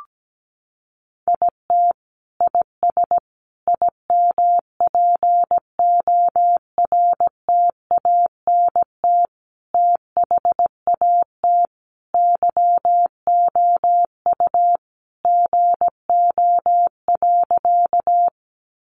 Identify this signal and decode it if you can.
Morse code, heard as a keyed tone, about 17 wpm, 700 Hz. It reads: IT IS IMPORTANT THAT YOU GO.
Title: Sentences from Top 600 Words 17wpm